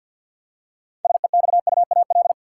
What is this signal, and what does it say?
Morse code, heard as a keyed tone, about 46 wpm, 700 Hz. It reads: SEXUAL
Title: Six Letter Words - 50wpm